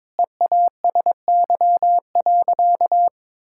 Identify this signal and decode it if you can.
Morse code, heard as a keyed tone, about 22 wpm, 700 Hz. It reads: EASY.
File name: Sentences with 7-Word Phrases - 22wpm